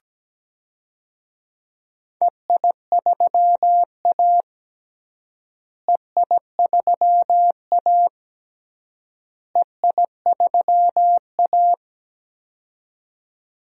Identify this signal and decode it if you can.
Morse code, heard as a keyed tone, about 17 wpm, 700 Hz. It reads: EI3A EI3A EI3A
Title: DXCC-100 Call Signs - 4 Characters - Repeated 3x - 17wpm